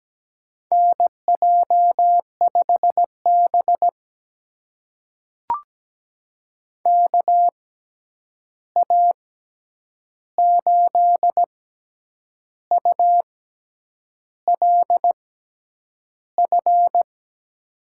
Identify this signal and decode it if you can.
Morse code, heard as a keyed tone, about 17 wpm, spaced as for 8 wpm, 700 Hz. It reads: NJ5B KA8ULF
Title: USA Call Signs - 6x Character Spacing - 17wpm